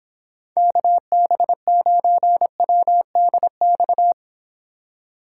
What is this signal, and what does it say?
Morse code, heard as a keyed tone, about 26 wpm, 700 Hz. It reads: KB9WDX